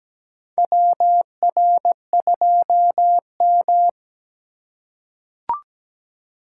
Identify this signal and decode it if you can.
Morse code, heard as a keyed tone, about 17 wpm, 700 Hz. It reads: WR2M
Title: USA Call Signs - 4x Character Spacing - 17wpm